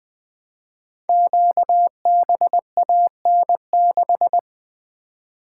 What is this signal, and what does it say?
Morse code, heard as a keyed tone, about 20 wpm, 700 Hz. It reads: QBAN6
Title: Code Groups - 5 Letter:Numbers - 20wpm